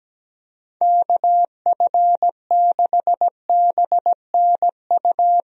Morse code, heard as a keyed tone, about 17 wpm, 700 Hz. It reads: KF6BNU
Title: USA Call Signs - 6 Characters - 17wpm